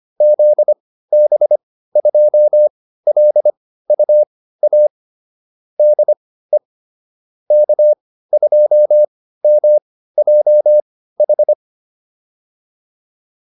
Morse code, heard as a keyed tone, about 25 wpm, spaced as for 15 wpm, 600 Hz. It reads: ZB2LUA DE K2MJH
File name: QSO402 15